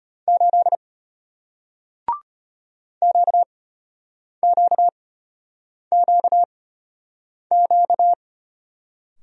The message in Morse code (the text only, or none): 8 QQQQ